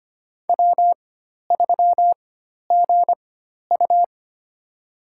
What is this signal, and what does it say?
Morse code, heard as a keyed tone, about 25 wpm, spaced as for 12 wpm, 700 Hz. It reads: W3GU
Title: USA Call Signs - 4x Character Spacing - 25wpm